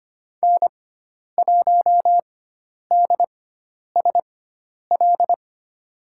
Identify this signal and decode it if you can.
Morse code, heard as a keyed tone, about 25 wpm, spaced as for 10 wpm, 700 Hz. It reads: N1DSL
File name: Lesson 16g - Letter c - Call Signs 25wpm